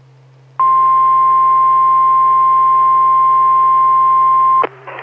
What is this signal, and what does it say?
Emergency Alert System (SAME) tones (the 1050 Hz attention tone). It is heard from 0.5 to 4.6 s.
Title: Saline, AK SKYWARN Repeater System - W5RHS